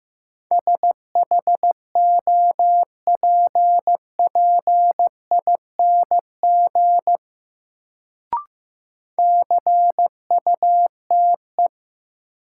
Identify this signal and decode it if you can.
Morse code, heard as a keyed tone, about 15 wpm, 700 Hz. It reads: SHOPPING CUTE
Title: Sentences from Top 1000 Words - Review Part 7 - New Word Drill - 15wpm